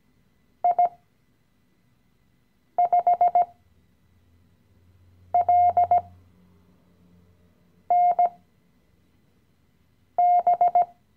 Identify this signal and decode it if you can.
Morse code, heard as a keyed tone, about 17 wpm, spaced as for 4 wpm, 710 Hz. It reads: I5LNB
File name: CW-Online-Kurs / Anfänger / Lektion 015